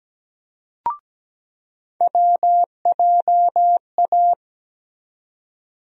Morse code, heard as a keyed tone, about 17 wpm, 700 Hz. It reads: WJA
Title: USA Call Sign Postfixes - 3 Letters 17wpm